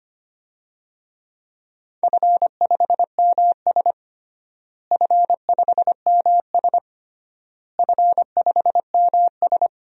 Morse code, heard as a keyed tone, about 25 wpm, 700 Hz. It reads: F5MS F5MS F5MS
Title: DXCC-100 Call Signs - 4 Characters - Repeated 3x - 25wpm